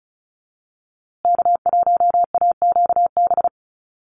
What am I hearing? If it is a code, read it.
Morse code, heard as a keyed tone, about 34 wpm, 700 Hz. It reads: K1AQB
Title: USA Call Signs Repeated 3x 35wpm